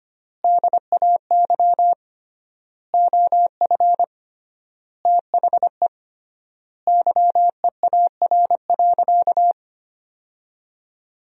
Morse code, heard as a keyed tone, about 25 wpm, 700 Hz. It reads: DAY OF THE YEAR.
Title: 3x Word-Spacing - Sentences from Top 300 Words - 25wpm